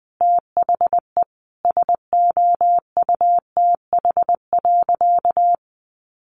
Morse code, heard as a keyed tone, about 20 wpm, 700 Hz. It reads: THE SOUTH.
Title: Sentences from Top 1500 Words 20wpm